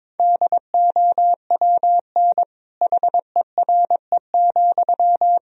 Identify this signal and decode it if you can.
Morse code, heard as a keyed tone, about 21 wpm, 700 Hz. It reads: DOWN HERE,
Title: Sentences from Top 1000 Words - Review Part 1 - 22wpm